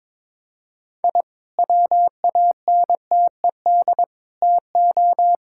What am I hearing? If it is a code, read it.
Morse code, heard as a keyed tone, about 22 wpm, 700 Hz. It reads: I WANTED TO
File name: practice / 22wpm / Sentences with 3-Word Phrases - 22wpm